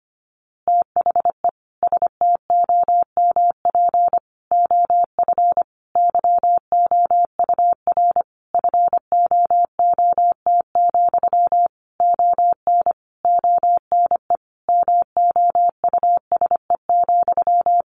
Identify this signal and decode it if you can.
Morse code, heard as a keyed tone, about 25 wpm, 700 Hz. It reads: THE STOMP OF YOUR FOOT, ON ONE MOUSE,
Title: A Sound Of Thunder 25wpm